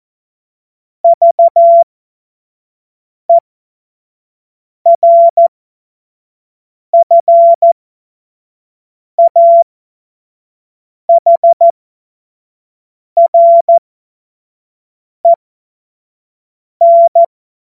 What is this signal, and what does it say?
Morse code, heard as a keyed tone, about 14 wpm, spaced as for 5 wpm, 680 Hz. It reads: VERFAHREN
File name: Morsekurs / Witze / Witz 23. August 2022 25 F70